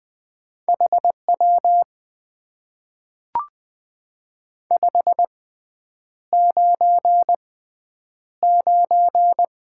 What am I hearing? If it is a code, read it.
Morse code, heard as a keyed tone, about 20 wpm, spaced as for 9 wpm, 700 Hz. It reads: HW 599
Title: Top 90 QSO Elements - 6x Character Spacing - 20wpm